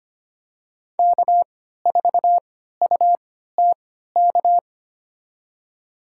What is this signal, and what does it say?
Morse code, heard as a keyed tone, about 25 wpm, spaced as for 14 wpm, 700 Hz. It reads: K4UTK
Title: USA Call Signs - 5 Characters - 3x Character Spacing - 25wpm